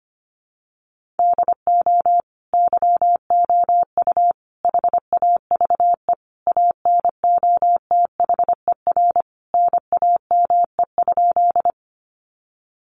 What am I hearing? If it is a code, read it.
Morse code, heard as a keyed tone, about 25 wpm, 700 Hz. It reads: DO YOU HAVE ANOTHER NAME?
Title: Sentences from Top 600 Words 25wpm